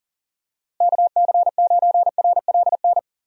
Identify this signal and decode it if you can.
Morse code, heard as a keyed tone, about 40 wpm, 700 Hz. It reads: KC9RLN
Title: USA Call Signs - 6 Characters - Repeated 2x - 40wpm